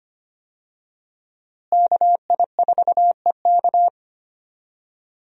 Morse code, heard as a keyed tone, about 25 wpm, 700 Hz. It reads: KI4EK
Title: USA Call Signs - 5 Characters - Repeated 3x - 25wpm